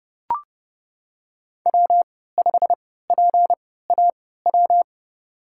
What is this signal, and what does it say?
Morse code, heard as a keyed tone, about 30 wpm, spaced as for 17 wpm, 700 Hz. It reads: W5PAW
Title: USA Call Signs - 5 Characters - 3x Character Spacing - 30wpm